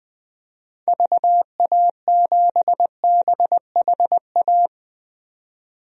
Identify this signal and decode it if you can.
Morse code, heard as a keyed tone, about 20 wpm, 700 Hz. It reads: VA7BHA
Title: DXCC-100 Call Signs - 6 Characters - 20wpm